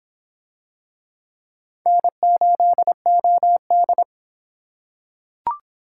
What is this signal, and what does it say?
Morse code, heard as a keyed tone, about 23 wpm, 700 Hz. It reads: N8OD